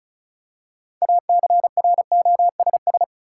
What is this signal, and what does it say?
Morse code, heard as a keyed tone, about 34 wpm, 700 Hz. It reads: ACROSS
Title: Six Letter Words - 35wpm